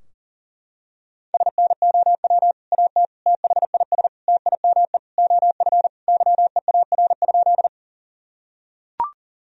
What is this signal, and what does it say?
Morse code, heard as a keyed tone, about 40 wpm, 700 Hz. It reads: SNOW AT THIS TIME OF YEAR?
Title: Sentences from Top 1000 Words - Review Part 3 - 40wpm